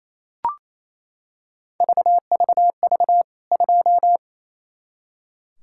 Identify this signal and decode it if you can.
Morse code, heard as a keyed tone, about 28 wpm, 700 Hz. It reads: VVV 2
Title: K1USN SST QSO Practice 28wpm